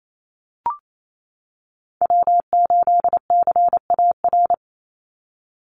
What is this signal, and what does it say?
Morse code, heard as a keyed tone, about 28 wpm, 700 Hz. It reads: W8CAR
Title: Call Signs - 28wpm